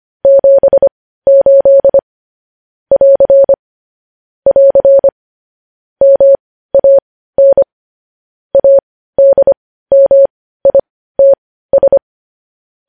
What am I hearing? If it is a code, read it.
Morse code, heard as a keyed tone, about 25 wpm, spaced as for 15 wpm, 550 Hz. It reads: 78 <AR> <AR> MAN ADMITS
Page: news headlines in morse code at 15 wpm